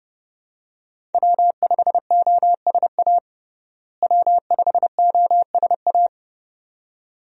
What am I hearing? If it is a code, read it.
Morse code, heard as a keyed tone, about 32 wpm, 700 Hz. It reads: W5OSA W5OSA
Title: USA Call Signs - 5 Characters - Repeated 2x - 30wpm